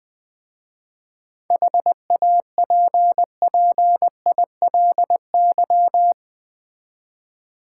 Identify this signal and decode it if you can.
Morse code, heard as a keyed tone, about 21 wpm, 700 Hz. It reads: HAPPILY
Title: 7 Letter Words - First 3 Letters Given - 20wpm